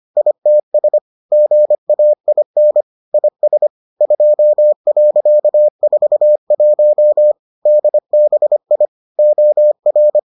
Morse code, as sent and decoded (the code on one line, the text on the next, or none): .. - ... / --. .- .. -. / .. ... / ..--- .-.-.- ....- .---- / -.. -... .. / --- .-.
ITS GAIN IS 2.41 DBI OR